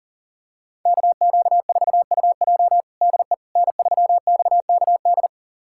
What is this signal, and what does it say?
Morse code, heard as a keyed tone, about 40 wpm, 700 Hz. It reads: KQ4UJ DE N3XKD
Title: USA Callsign DE Callsign - 5 Characters - 40wpm